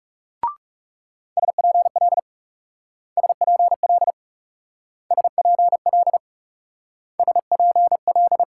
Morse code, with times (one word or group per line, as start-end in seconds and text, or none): 1.4-2.2 SPL
3.2-4.1 SPL
5.1-6.2 SPL
7.2-8.4 SPL